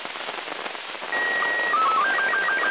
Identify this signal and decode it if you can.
SSTV (slow-scan television), Robot 36.
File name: ao-51-apollo-11